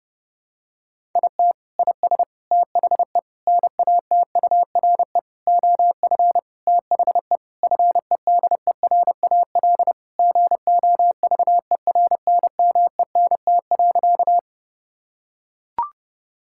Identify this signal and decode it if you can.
Morse code, heard as a keyed tone, about 32 wpm, 700 Hz. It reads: IT IS THE NATURE OF THE FEDERAL GOVERNMENT.
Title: Sentences from Top 1000 Words - Review Part 5 - 30wpm